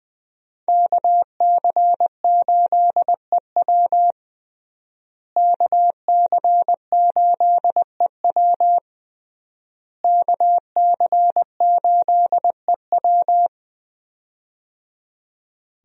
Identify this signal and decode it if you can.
Morse code, heard as a keyed tone, about 20 wpm, 700 Hz. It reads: KC8EW KC8EW KC8EW
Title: USA Call Signs - 5 Characters - Repeated 3x - 20wpm